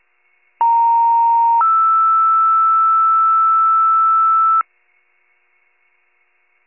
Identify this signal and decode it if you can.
Two-tone page, heard as a one-second tone, then a three-second tone, about 930 then 1320 Hz. It is heard from 0.6 to 4.6 s.